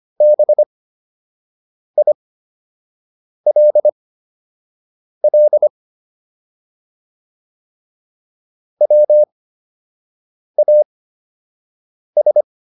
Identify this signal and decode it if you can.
Morse code, heard as a keyed tone, about 25 wpm, spaced as for 6 wpm, 600 Hz. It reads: BILL WAS